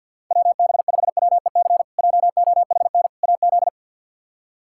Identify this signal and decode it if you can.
Morse code, heard as a keyed tone, about 49 wpm, 700 Hz. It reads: WB5WEC JOHN AZ
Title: K1USN SST QSO Practice - 50wpm